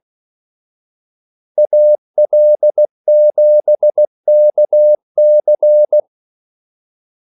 Morse code, heard as a keyed tone, about 16 wpm, 600 Hz. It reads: AL7KC
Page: Kaldesignaler 16 wpm